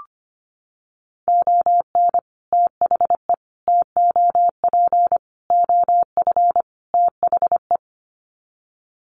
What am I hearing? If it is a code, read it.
Morse code, heard as a keyed tone, about 25 wpm, 700 Hz. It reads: ON THE TOP OF THE
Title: Sets of 5 Words Encore 25wpm